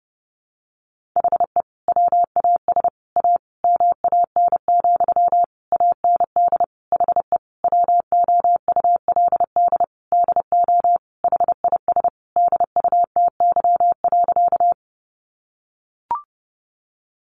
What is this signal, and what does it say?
Morse code, heard as a keyed tone, about 30 wpm, 700 Hz. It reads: HE WAS A MAN, AND HE WOULD DO HIS DUTY.
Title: Sentences from Top 1500 Words 30wpm